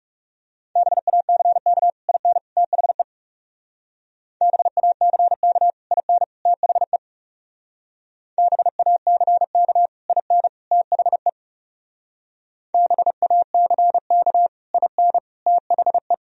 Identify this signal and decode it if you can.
Morse code, heard as a keyed tone, about 35 wpm, 700 Hz. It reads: BACK IN THE BACK IN THE BACK IN THE BACK IN THE
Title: Speed-Racing - Sets of 3 Words 30wpm